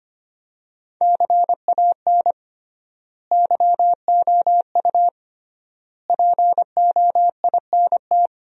Morse code, heard as a keyed tone, about 25 wpm, 700 Hz. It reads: CAN YOU POINT
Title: 3x Word-Spacing - Sentences from Top 200 Words - 25wpm